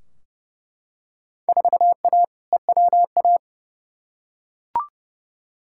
Morse code, heard as a keyed tone, about 30 wpm, 700 Hz. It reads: 4A EWA
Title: Field Day Exchanges - 30wpm